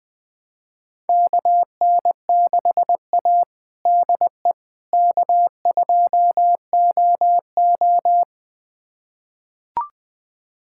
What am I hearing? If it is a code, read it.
Morse code, heard as a keyed tone, about 20 wpm, 700 Hz. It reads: KN6A DE K2OO
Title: USA Callsign DE Callsign - 4 Characters - 20wpm